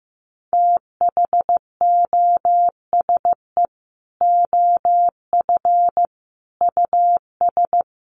Morse code, heard as a keyed tone, about 15 wpm, 700 Hz. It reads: THOSE OF US
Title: Sets of 5 Words Encore2 15wpm